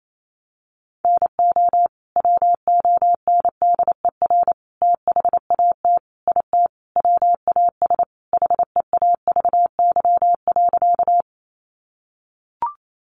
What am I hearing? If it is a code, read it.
Morse code, heard as a keyed tone, about 28 wpm, 700 Hz. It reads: NO WONDER THAT IT WAS HEAVY.
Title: Sentences from Top 1100 Words 28wpm